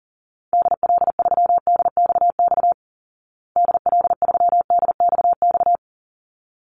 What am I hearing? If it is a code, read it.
Morse code, heard as a keyed tone, about 40 wpm, 700 Hz. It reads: DL3DXX DL3DXX
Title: Call Signs Repeated 3x - 40wpm